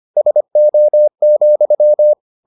Morse code, heard as a keyed tone, about 25 wpm, 600 Hz. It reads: SO,